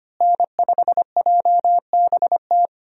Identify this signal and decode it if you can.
Morse code, heard as a keyed tone, about 25 wpm, 700 Hz. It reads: N5JBT
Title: USA Call Signs - 5 Characters - Repeated 3x - 25wpm